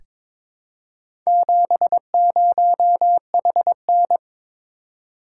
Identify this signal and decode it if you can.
Morse code, heard as a keyed tone, about 22 wpm, 700 Hz. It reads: 70HN
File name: practice / 22wpm / Four Letter-Number Combinations - 22wpm